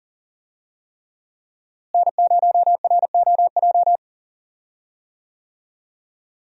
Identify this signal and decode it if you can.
Morse code, heard as a keyed tone, about 40 wpm, 700 Hz. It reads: N0ROJ
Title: USA Call Signs - 5 Characters - Repeated 3x - 40wpm